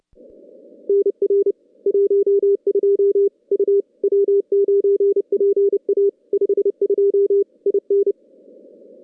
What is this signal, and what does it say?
Morse code, heard as a keyed tone, about 28 wpm, 410 Hz. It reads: NR 12 U W9PA 52 IN
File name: contests / CW-SS-k5tr- -2100